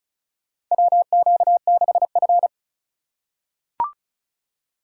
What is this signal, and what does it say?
Morse code, heard as a keyed tone, about 36 wpm, 700 Hz. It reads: WQ6F
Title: USA Call Signs - 4 Characters - 4x Character Spacing - 35wpm